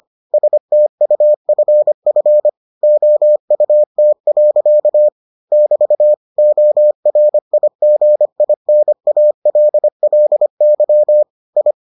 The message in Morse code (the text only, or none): STUFF OUT. <BT> ORIGINALLY I